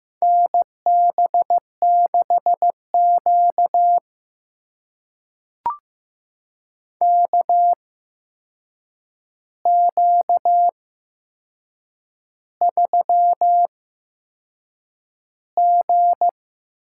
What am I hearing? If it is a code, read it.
Morse code, heard as a keyed tone, about 15 wpm, spaced as for 6 wpm, 700 Hz. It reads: NB6Q KQ3G
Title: USA Call Signs - 4 Characters - 8x Character Spacing - 15wpm